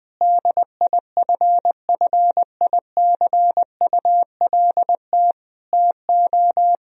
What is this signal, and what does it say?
Morse code, heard as a keyed tone, about 19 wpm, 700 Hz. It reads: DIFFICULT TO
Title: Sentences from Top 1000 Words - Review Part 3 - 20wpm